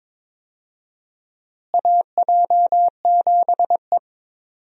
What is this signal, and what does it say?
Morse code, heard as a keyed tone, about 22 wpm, 700 Hz. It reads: AJ7E
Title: USA Call Signs - 4 Characters - Repeated 2x - 22wpm